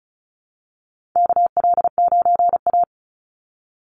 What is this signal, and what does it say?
Morse code, heard as a keyed tone, about 34 wpm, 700 Hz. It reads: KL9A
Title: Call Signs Repeated 3x - 35wpm